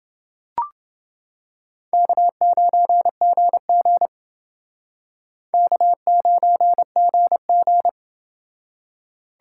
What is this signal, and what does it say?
Morse code, heard as a keyed tone, about 28 wpm, 690 Hz. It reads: K9GG K9GG